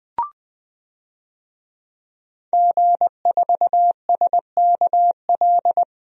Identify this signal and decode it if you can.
Morse code, heard as a keyed tone, about 20 wpm, 700 Hz. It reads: G4SKL
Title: DXCC-100 Call Signs - 5 Characters - Repeated 3x - 20wpm